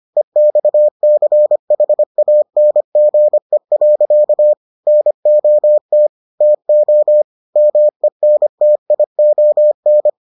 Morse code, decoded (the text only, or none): EXCHANGE. NOT TO MENTION